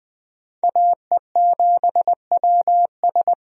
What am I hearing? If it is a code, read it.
Morse code, heard as a keyed tone, about 20 wpm, 700 Hz. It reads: AE7WS
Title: USA Call Signs - 5 Characters - Repeated 2x - 20wpm